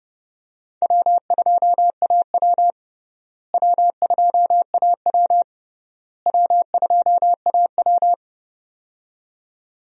Morse code, heard as a keyed tone, about 30 wpm, 700 Hz. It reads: W2AW W2AW W2AW
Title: USA Call Signs - 4 Characters - Repeated 3x - 30wpm